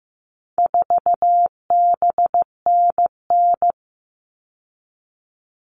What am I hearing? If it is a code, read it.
Morse code, heard as a keyed tone, about 15 wpm, 700 Hz. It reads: 4BNN
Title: All Cut Numbers - 4-Digit 15wpm